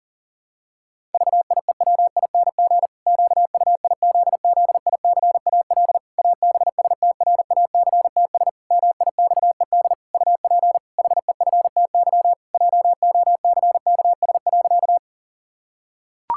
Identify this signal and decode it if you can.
Morse code, heard as a keyed tone, about 40 wpm, 700 Hz. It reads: VIEWING QUIZZICAL ABSTRACTS MIXED UP HEFTY JOCKS.
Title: Pangram Challenge v2 - 40wpm